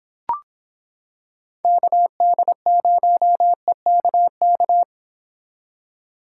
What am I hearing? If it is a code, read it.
Morse code, heard as a keyed tone, about 26 wpm, 700 Hz. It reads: KD0EKK